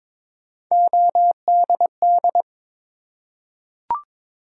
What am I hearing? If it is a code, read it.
Morse code, heard as a keyed tone, about 22 wpm, 700 Hz. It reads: ODD
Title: Three Letter Words - 22wpm